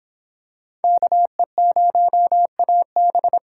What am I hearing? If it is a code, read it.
Morse code, heard as a keyed tone, about 26 wpm, 700 Hz. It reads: KE0AB